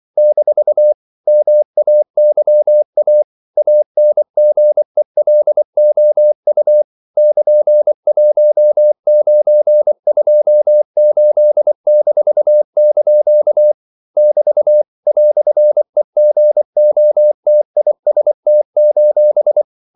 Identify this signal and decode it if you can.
Morse code, heard as a keyed tone, about 24 wpm, 600 Hz. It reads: - MAYA ANGELOU <KN>1928-) <BT> "EGOTIST:
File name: CW / Practice / EnglishQuotations0224